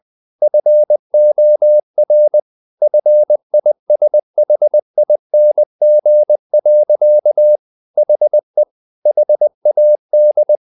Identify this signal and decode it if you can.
Morse code, heard as a keyed tone, about 20 wpm, 600 Hz. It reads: FOR FISHING. HE HAD
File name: the old man and the sea part 2 20wpm